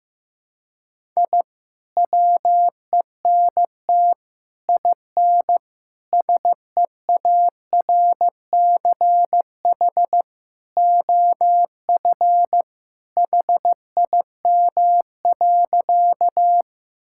Morse code, heard as a keyed tone, about 15 wpm, 700 Hz. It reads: I WENT IN SEARCH OF HIM.
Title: Sentences from Top 1000 Words - Review Part 5 - 15wpm